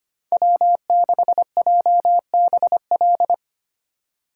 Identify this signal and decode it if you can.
Morse code, heard as a keyed tone, about 28 wpm, 700 Hz. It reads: W6JBL
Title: Speed Racing - USA 1x3 Call Signs 25wpm